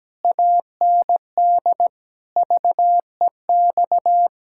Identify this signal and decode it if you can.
Morse code, heard as a keyed tone, about 17 wpm, 700 Hz. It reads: AND VEX
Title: Pangram Challenge v2 - 17wpm